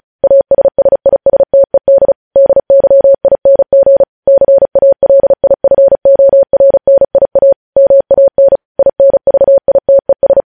Morse code, heard as a keyed tone, about 36 wpm, 550 Hz. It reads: ASSISTED DYING CALIFORNIA MAN INVITES
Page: news headlines in morse code at 35 wpm